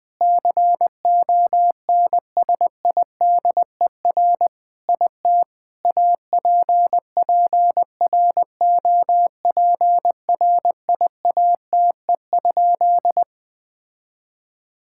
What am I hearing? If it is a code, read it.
Morse code, heard as a keyed tone, about 20 wpm, 700 Hz. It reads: CONSIDER IT APPROPRIATE?
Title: Sentences from Top 1000 Words - Review Part 5 - 20wpm